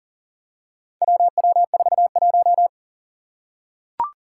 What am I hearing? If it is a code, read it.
Morse code, heard as a keyed tone, about 40 wpm, 700 Hz. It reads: WW41
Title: Four Challenging Letter-Number Combinations 40wpm